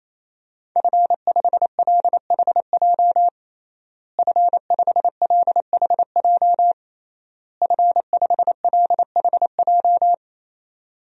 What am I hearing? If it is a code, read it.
Morse code, heard as a keyed tone, about 28 wpm, 700 Hz. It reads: F5LHJ F5LHJ F5LHJ
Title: DXCC-100 Call Signs - 5 Characters - Repeated 3x - 28wpm